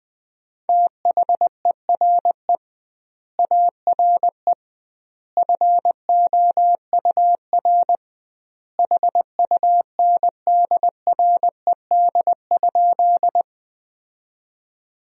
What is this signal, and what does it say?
Morse code, heard as a keyed tone, about 21 wpm, 700 Hz. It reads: THERE ARE FOUR HUNDRED?
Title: 2x Word-Spacing - Sentences from Top 400 Words - 20wpm